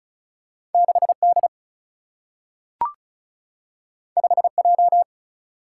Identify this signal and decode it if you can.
Morse code, heard as a keyed tone, about 36 wpm, 700 Hz. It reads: 6D 5J
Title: Single-Letter Single-Number 35wpm